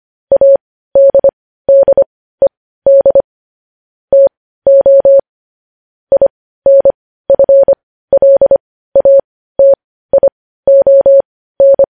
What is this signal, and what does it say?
Morse code, heard as a keyed tone, about 25 wpm, spaced as for 15 wpm, 550 Hz. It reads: ADDED TO INFLATION
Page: news headlines in morse code at 15 wpm